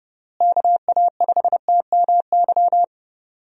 Morse code, heard as a keyed tone, about 30 wpm, 700 Hz. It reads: KA5TMY
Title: USA Call Signs - 6 Characters - Repeated 3x - 30wpm